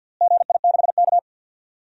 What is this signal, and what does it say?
Morse code, heard as a keyed tone, about 47 wpm, 700 Hz. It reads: GI6K
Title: DXCC-100 Call Signs - 4 Characters - Repeated 2x - 50wpm